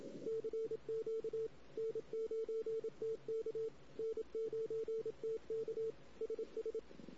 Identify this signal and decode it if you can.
Morse code, heard as a keyed tone, about 27 wpm, 440 Hz. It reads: CQ N9TK N9TK SS
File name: CW / 2008-ss-cw-k5tr-op-n5rz -16:00